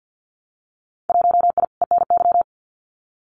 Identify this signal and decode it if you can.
Morse code, heard as a keyed tone, about 50 wpm, 700 Hz. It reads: 1I ENY
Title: Winter Field Day Exchanges Encore 50wpm